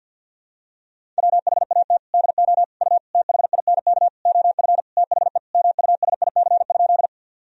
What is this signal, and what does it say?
Morse code, heard as a keyed tone, about 47 wpm, 700 Hz. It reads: WHAT DO U THINK OF THE MUSIC?
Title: Sentences from Top 300 Words and Abbreviations 50wpm